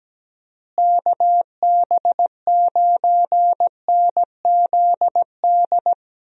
Morse code, heard as a keyed tone, about 17 wpm, 700 Hz. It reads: KB9NZD
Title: USA Call Signs - 6 Characters - Repeated 2x - 17wpm